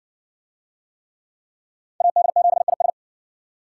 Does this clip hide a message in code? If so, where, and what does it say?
Yes: Morse code from 2.0 to 2.9 s, heard as a keyed tone, about 60 wpm, 700 Hz. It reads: AD7IS